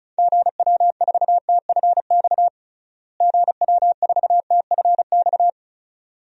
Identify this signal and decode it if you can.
Morse code, heard as a keyed tone, about 34 wpm, 700 Hz. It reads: GW4TFX GW4TFX
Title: DXCC-100 Call Signs - 6 Characters - Repeated 2x - 35wpm